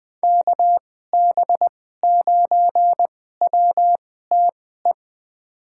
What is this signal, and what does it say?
Morse code, heard as a keyed tone, about 20 wpm, spaced as for 14 wpm, 700 Hz. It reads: KB9WTE
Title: USA Call Signs - 6 Characters - 2x Character Spacing - 20wpm